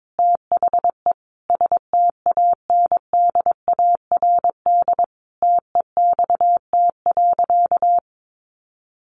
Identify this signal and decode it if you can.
Morse code, heard as a keyed tone, about 22 wpm, 700 Hz. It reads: THE STANDARD TEXT.
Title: Sentences from Top 1200 Words 22wpm